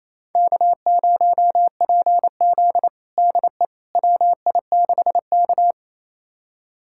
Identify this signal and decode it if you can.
Morse code, heard as a keyed tone, about 28 wpm, 700 Hz. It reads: K0PZ DE WI6K
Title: USA Callsign DE Callsign - 4 Characters - 28wpm